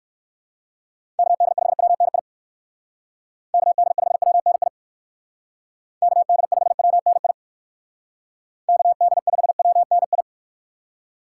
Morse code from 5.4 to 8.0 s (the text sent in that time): KD5WNI